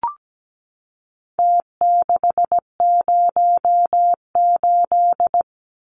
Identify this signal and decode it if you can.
Morse code, heard as a keyed tone, about 17 wpm, 700 Hz. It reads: T608
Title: Numbers - Contest Style - 4-Digit 17wpm